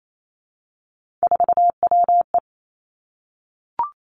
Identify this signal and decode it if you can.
Morse code, heard as a keyed tone, about 28 wpm, 700 Hz. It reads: 4WE
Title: Three Challenging Letter-Number Combinations 28wpm